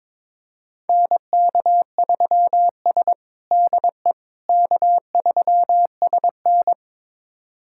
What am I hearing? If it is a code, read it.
Morse code, heard as a keyed tone, about 22 wpm, 700 Hz. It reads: NK3S DE K3SN
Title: USA Callsign DE Callsign - 4 Characters - 22wpm